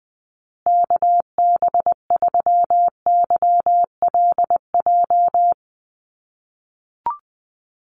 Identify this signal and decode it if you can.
Morse code, heard as a keyed tone, about 20 wpm, 700 Hz. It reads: KB3YLJ
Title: USA Call Signs Repeated 3x 20wpm